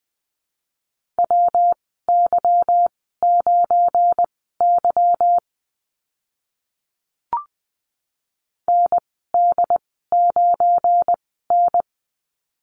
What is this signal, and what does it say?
Morse code, heard as a keyed tone, about 20 wpm, spaced as for 14 wpm, 700 Hz. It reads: WY9Y ND9N
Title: 186 Lesson 30g - Letter b - Call Signs 20wpm